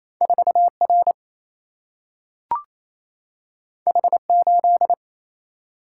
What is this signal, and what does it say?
Morse code, heard as a keyed tone, about 28 wpm, 700 Hz. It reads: 4R H8
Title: Single-Letter Single-Number 28wpm